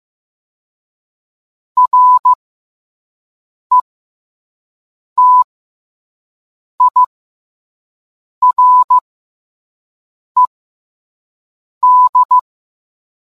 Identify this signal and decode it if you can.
Morse code, heard as a keyed tone, about 15 wpm, spaced as for 5 wpm, 1000 Hz. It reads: RETIRED